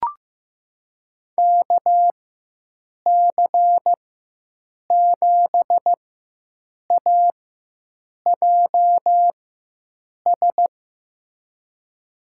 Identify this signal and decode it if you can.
Morse code, heard as a keyed tone, about 15 wpm, spaced as for 7 wpm, 700 Hz. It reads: KC7AJS
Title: USA Call Signs - 6 Characters - 4x Character Spacing - 15wpm